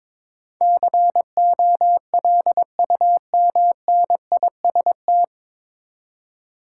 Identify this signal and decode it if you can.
Morse code, heard as a keyed tone, about 22 wpm, 700 Hz. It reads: COLUMNIST
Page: Morse Code Ninja practice 22wpm